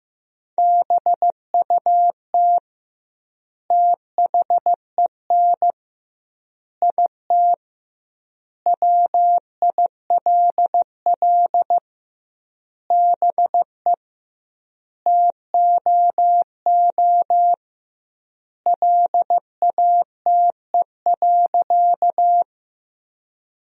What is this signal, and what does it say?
Morse code, heard as a keyed tone, about 15 wpm, 700 Hz. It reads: BUT THEN IT WILL BE TOO LATE.
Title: 2x Word-Spacing - Sentences from Top 300 Words - 15wpm